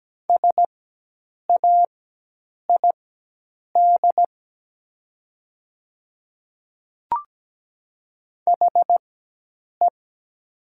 Morse code, heard as a keyed tone, about 17 wpm, spaced as for 8 wpm, 700 Hz. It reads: SAID HE
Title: 4x Character Spacing - Top 500 Words - 17wpm